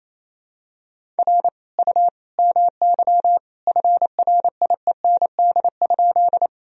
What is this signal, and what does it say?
Morse code, heard as a keyed tone, about 28 wpm, 700 Hz. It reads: R U MY FRIEND?
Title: Sentences from Top 300 Words and Abbreviations 28wpm